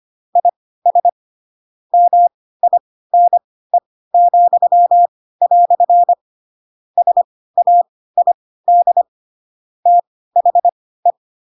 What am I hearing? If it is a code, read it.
Morse code, heard as a keyed tone, about 25 wpm, spaced as for 16 wpm, 700 Hz. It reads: IS MINE," SAID THE
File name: Pinocchio / Pinocchio ch4 10 25 ik1hge